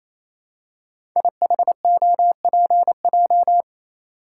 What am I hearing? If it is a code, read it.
Morse code, heard as a keyed tone, about 27 wpm, 700 Hz. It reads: IHOPJ
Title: Code Groups - 5 Letters - 28wpm